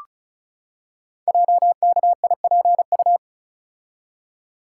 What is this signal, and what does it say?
Morse code, heard as a keyed tone, about 36 wpm, 700 Hz. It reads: JKIPU
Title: Code Groups - 5 Letters - 35wpm